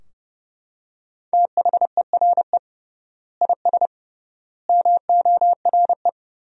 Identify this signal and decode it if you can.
Morse code, heard as a keyed tone, about 30 wpm, 700 Hz. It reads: THERE IS MORE
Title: 3x Word-Spacing - Sentences from Top 200 Words - 30wpm